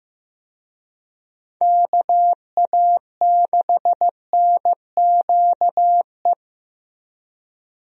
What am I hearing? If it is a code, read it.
Morse code, heard as a keyed tone, about 15 wpm, 700 Hz. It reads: KA6NQE
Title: USA Call Signs - 6 Characters - Repeated 3x - 15wpm